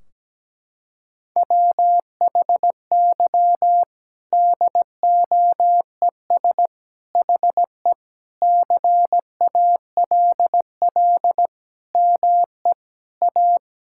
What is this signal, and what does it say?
Morse code, heard as a keyed tone, about 17 wpm, 700 Hz. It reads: WHY DOES HE CALL ME A
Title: Sentences from Top 1600 Words 17wpm